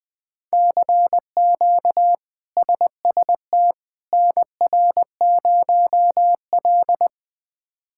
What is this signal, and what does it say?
Morse code, heard as a keyed tone, about 20 wpm, 700 Hz. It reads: CQ SST NR0L
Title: K1USN SST QSO Practice - 20wpm